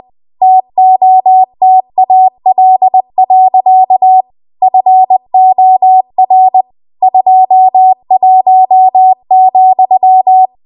Morse code, heard as a keyed tone, about 20 wpm, 750 Hz. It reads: TOTAL. FOR 21,
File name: Archive / 20 WPM / 20WPM